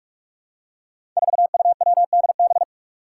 Morse code, heard as a keyed tone, about 46 wpm, 700 Hz. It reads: 4UWDB
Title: Five Challenging Letter-Number Combinations 45wpm